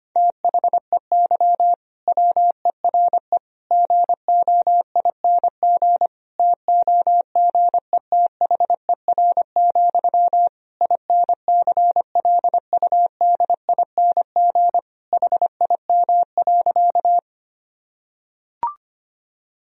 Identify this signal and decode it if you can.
Morse code, heard as a keyed tone, about 25 wpm, 700 Hz. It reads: THEY WERE GOING TOGETHER, INCLUDING HIM.
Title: Sentences from Top 1000 Words - Review Part 4 - 25wpm